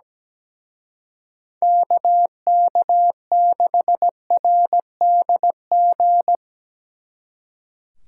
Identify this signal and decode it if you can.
Morse code, heard as a keyed tone, about 19 wpm, 700 Hz. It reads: KK6RDG
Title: Speed Racing - USA 2x3 Call Signs 17wpm